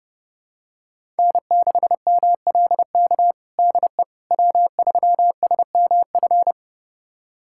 Morse code, heard as a keyed tone, about 32 wpm, 700 Hz. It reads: N6MLK DE W3SMF
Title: USA Callsign DE Callsign - 5 Characters - 30wpm